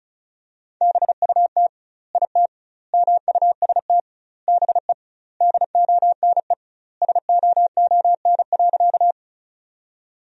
Morse code, heard as a keyed tone, about 36 wpm, 700 Hz. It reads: BUT IT MUST BE DONE SOON.
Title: 2x Word-Spacing - Sentences from Top 500 Words - 35wpm